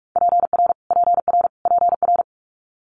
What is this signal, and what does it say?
Morse code, heard as a keyed tone, about 46 wpm, 700 Hz. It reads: PR PR PR
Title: Multi-letter Phonemes Repeated 3x 45wpm